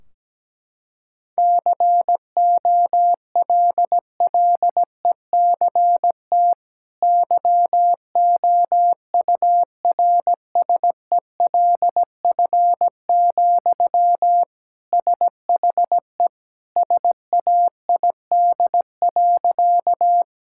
Morse code, heard as a keyed tone, about 17 wpm, 700 Hz. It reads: COLLECT YOURSELF, SHE SAID.
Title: Sentences from Top 1600 Words 17wpm